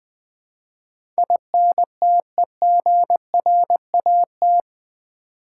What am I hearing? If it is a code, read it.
Morse code, heard as a keyed tone, about 20 wpm, 700 Hz. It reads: INTEGRAT
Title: Build A Word - 4 Letters At A Time - 11-Letter Words - 20wpm